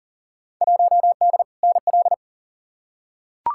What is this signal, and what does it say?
Morse code, heard as a keyed tone, about 40 wpm, 700 Hz. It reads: 1D NL
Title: Field Day Exchanges Encore - 40wpm